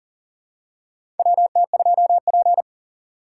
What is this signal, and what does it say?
Morse code, heard as a keyed tone, about 40 wpm, 700 Hz. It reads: WT2P
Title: USA Call Signs - 4 Characters - 2x Character Spacing - 40wpm